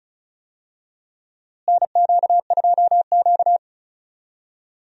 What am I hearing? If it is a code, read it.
Morse code, heard as a keyed tone, about 34 wpm, 700 Hz. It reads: NQ2Q
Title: USA Call Signs - 4 Characters - Repeated 3x - 35wpm